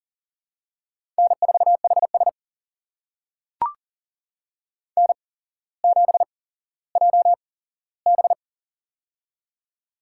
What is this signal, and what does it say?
Morse code, heard as a keyed tone, about 40 wpm, spaced as for 17 wpm, 700 Hz. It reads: N4HS N7JB
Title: USA Call Signs - 4 Characters - 8x Character Spacing - 40wpm